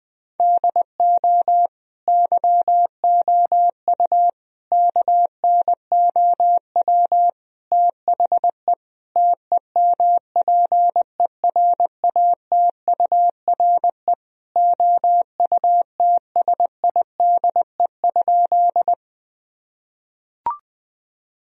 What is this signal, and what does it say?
Morse code, heard as a keyed tone, about 20 wpm, 700 Hz. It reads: DO YOU KNOW THE TEMPERATURE OUTSIDE?
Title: Sentences from Top 1000 Words - Review Part 4 - 20wpm